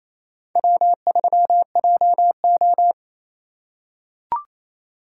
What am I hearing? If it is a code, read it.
Morse code, heard as a keyed tone, about 28 wpm, 700 Hz. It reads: W3JO
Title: USA Call Signs - 4 Characters - 6x Character Spacing - 28wpm